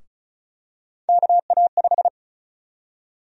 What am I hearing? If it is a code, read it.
Morse code, heard as a keyed tone, about 36 wpm, 700 Hz. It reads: KA5
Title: USA Call Sign Prefixes - Repeated 2x - 35wpm